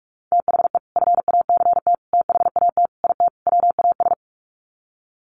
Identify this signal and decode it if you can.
Morse code, heard as a keyed tone, about 44 wpm, 700 Hz. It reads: THE FACT THAT IT WAS
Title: Sets of 5 Words Encore2 45wpm